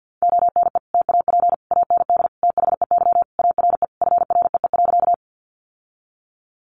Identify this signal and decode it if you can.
Morse code, heard as a keyed tone, about 50 wpm, 700 Hz. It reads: ONE TAP AND THEY ARE FREE.
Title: Sentences from Top 1400 Words 50wpm